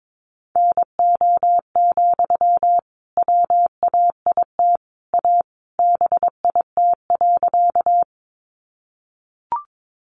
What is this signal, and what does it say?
Morse code, heard as a keyed tone, about 22 wpm, 700 Hz. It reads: NO, WAIT A BIT.
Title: Sentences from Top 900 Words 22wpm